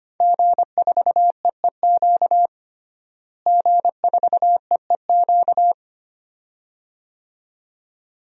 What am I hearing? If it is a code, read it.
Morse code, heard as a keyed tone, about 25 wpm, 700 Hz. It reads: G4EEQ G4EEQ
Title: DXCC-100 Call Signs - 5 Characters - Repeated 2x - 25wpm